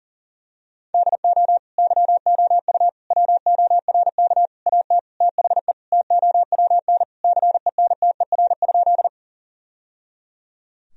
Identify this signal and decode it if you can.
Morse code, heard as a keyed tone, about 40 wpm, 700 Hz. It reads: DO YOU WORK AT THE TOWN CENTER?
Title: Sentences from Top 1000 Words - Review Part 2 - 40wpm